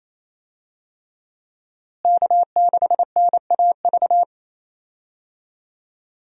Morse code, heard as a keyed tone, about 27 wpm, 700 Hz. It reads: K6NAV
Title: USA Call Signs - 5 Characters - Repeated 3x - 28wpm